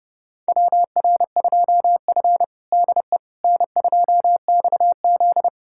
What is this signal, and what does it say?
Morse code, heard as a keyed tone, about 30 wpm, 700 Hz. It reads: WR2F DE N2XZ
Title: USA Callsign DE Callsign - 4 Characters - 30wpm